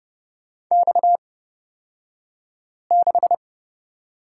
Morse code, heard as a keyed tone, about 30 wpm, spaced as for 5 wpm, 700 Hz. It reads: X6
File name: Lesson 37d - Letter x - Two Characters 30wpm